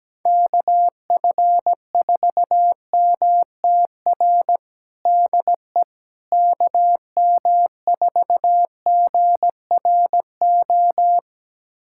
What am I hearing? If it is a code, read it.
Morse code, heard as a keyed tone, about 17 wpm, 700 Hz. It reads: KF4MTR DE KM4GRO
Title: USA Callsign DE Callsign 17wpm